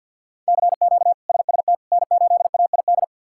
Morse code, heard as a keyed tone, about 46 wpm, 700 Hz. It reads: CQ SST N8AID
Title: K1USN SST QSO Practice - 50wpm